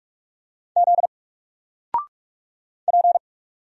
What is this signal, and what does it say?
Morse code, heard as a keyed tone, about 44 wpm, 700 Hz. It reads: Z P
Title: Single Character (Letter:Number:Punctuation:Prosign) 45wpm